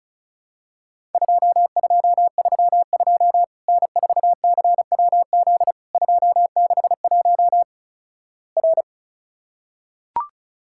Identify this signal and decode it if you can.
Morse code, heard as a keyed tone, about 36 wpm, 700 Hz. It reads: 2232 N4CWZ 261 R
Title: Worked All Europe DX Contest - QTC Exchange - 35wpm